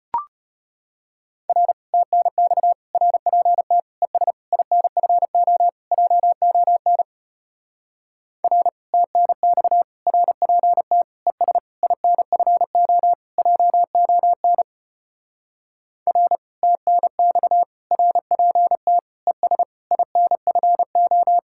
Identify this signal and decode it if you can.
Morse code, heard as a keyed tone, about 32 wpm, 700 Hz. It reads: R TNX RPT ES INFO JON R TNX RPT ES INFO JON R TNX RPT ES INFO